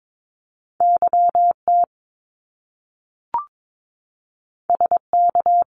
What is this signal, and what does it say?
Morse code, heard as a keyed tone, about 22 wpm, 700 Hz. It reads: YT SK
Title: Canadian Provinces and Territories 22wpm